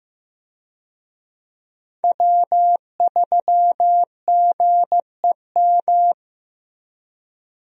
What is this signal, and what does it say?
Morse code, heard as a keyed tone, about 15 wpm, 700 Hz. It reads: W3GEM
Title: USA Call Signs - 5 Characters - Repeated 2x - 15wpm